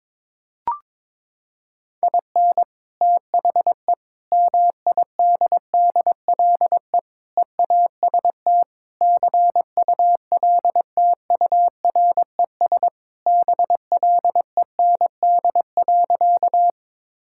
Morse code, heard as a keyed tone, about 21 wpm, 700 Hz. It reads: IN THE MIDDLE EAST CULTURES BLEND.
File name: practice / 22wpm / Sentences with 4-Word Phrases - 22wpm